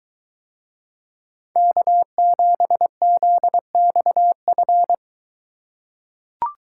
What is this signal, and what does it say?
Morse code, heard as a keyed tone, about 21 wpm, 700 Hz. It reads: K7ZXF